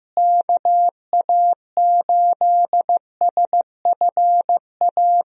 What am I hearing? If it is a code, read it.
Morse code, heard as a keyed tone, about 15 wpm, 700 Hz. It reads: KA8SFA
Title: USA Call Signs - 6 Characters - Repeated 2x - 15wpm